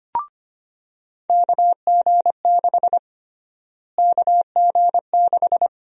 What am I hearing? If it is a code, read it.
Morse code, heard as a keyed tone, about 25 wpm, 700 Hz. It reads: KG6 KG6
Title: USA Call Sign Prefixes - Repeated 2x - 25wpm